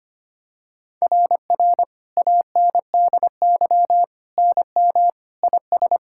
Morse code, heard as a keyed tone, about 25 wpm, 700 Hz. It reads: RR ANDY NM IS
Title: SOTA QSO Practice 25wpm